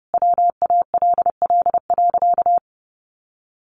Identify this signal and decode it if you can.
Morse code, heard as a keyed tone, about 30 wpm, 700 Hz. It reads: WALL.
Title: Sentences from Top 1100 Words 30wpm